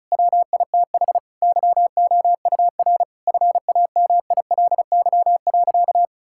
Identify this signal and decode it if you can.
Morse code, heard as a keyed tone, about 36 wpm, 700 Hz. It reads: WITH YOUR FAMILY.
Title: Sentences from Top 1600 Words 35wpm